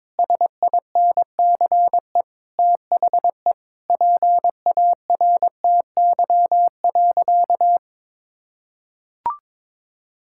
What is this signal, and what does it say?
Morse code, heard as a keyed tone, about 22 wpm, 700 Hz. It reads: SINCE THE PARTY.
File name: practice / 22wpm / Sentences with 2-Word Phrases - 22wpm